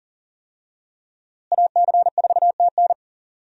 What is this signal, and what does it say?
Morse code, heard as a keyed tone, about 40 wpm, 700 Hz. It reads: AC4TN
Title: USA Call Signs - 5 Characters - Repeated 3x - 40wpm